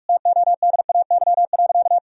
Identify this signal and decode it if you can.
Morse code, heard as a keyed tone, about 46 wpm, 700 Hz. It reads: TODAY.
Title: Sentences from Top 1000 Words - Review Part 4 - 45wpm